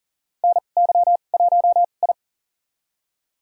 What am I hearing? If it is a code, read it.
Morse code, heard as a keyed tone, about 40 wpm, spaced as for 29 wpm, 700 Hz. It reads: NY1I
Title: USA Call Signs - 2x Character Spacing - 40wpm